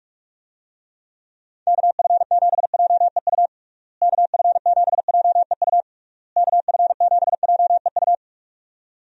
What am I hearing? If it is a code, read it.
Morse code, heard as a keyed tone, about 46 wpm, 700 Hz. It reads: KF7JEU KF7JEU KF7JEU
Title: USA Call Signs - 6 Characters - Repeated 3x - 45wpm